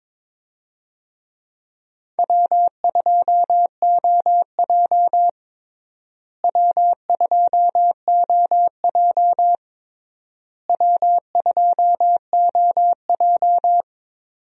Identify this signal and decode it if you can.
Morse code, heard as a keyed tone, about 22 wpm, 700 Hz. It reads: W2OJ W2OJ W2OJ
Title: USA Call Signs - 4 Characters - Repeated 3x - 22wpm